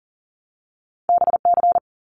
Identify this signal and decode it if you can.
Morse code, heard as a keyed tone, about 42 wpm, 700 Hz. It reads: BC
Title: Canadian Provinces and Territories 40wpm